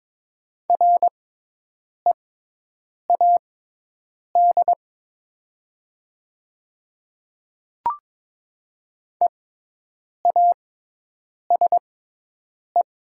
Morse code, heard as a keyed tone, about 22 wpm, spaced as for 8 wpm, 700 Hz. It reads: READ EASE
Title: 6x Character Spacing - Top 300 Words - 22wpm